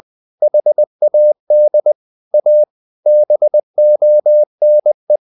Morse code, heard as a keyed tone, about 20 wpm, 600 Hz. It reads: HAD A BONE
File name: the old man and the sea / the old man and the sea part 1 20wpm